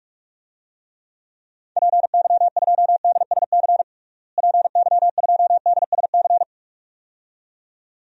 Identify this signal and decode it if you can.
Morse code, heard as a keyed tone, about 46 wpm, 700 Hz. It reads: PY2DSC PY2DSC
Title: DXCC-100 Call Signs - 6 Characters - Repeated 2x - 45wpm